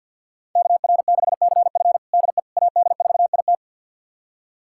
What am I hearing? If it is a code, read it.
Morse code, heard as a keyed tone, about 49 wpm, 700 Hz. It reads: KR6CF DE AD4IT
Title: USA Callsign DE Callsign - 5 Characters - 50wpm